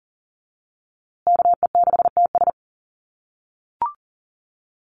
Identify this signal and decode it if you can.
Morse code, heard as a keyed tone, about 40 wpm, 700 Hz. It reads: KE6TS
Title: USA Call Signs Repeated 3x 40wpm